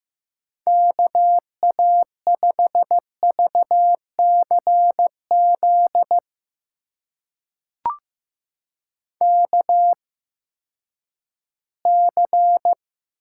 Morse code, heard as a keyed tone, about 15 wpm, spaced as for 10 wpm, 700 Hz. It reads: KA5VCZ KC
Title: USA Call Signs - 6 Characters - 8x Character Spacing - 15wpm